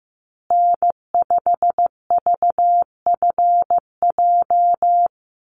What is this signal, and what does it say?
Morse code, heard as a keyed tone, about 15 wpm, 700 Hz. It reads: N5VFJ
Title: USA Call Signs Repeated 3x 15wpm